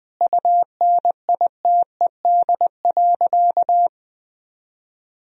Morse code, heard as a keyed tone, about 21 wpm, 700 Hz. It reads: UNITED.
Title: Sentences from Top 1000 Words - Review Part 4 - 20wpm